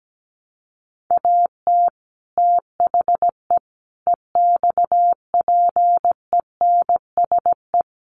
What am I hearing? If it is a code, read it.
Morse code, heard as a keyed tone, about 17 wpm, 700 Hz. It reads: AT THE EXPENSE
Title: Sets of 4 Words 17wpm